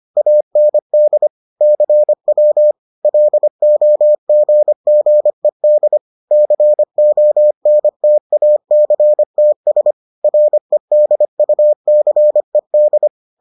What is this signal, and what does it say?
Morse code, heard as a keyed tone, about 25 wpm, 600 Hz. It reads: AND CW LOGGED CONTACTS REDUCED